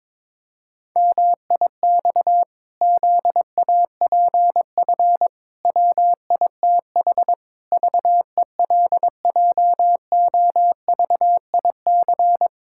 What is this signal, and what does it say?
Morse code, heard as a keyed tone, about 22 wpm, 700 Hz. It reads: MIX ZAPF WITH VELJOVIC
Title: Pangram Challenge v2 - 22wpm